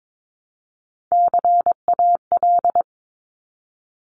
Morse code, heard as a keyed tone, about 22 wpm, 700 Hz. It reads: CAL
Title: Top Three Letter Combinations in Words 22wpm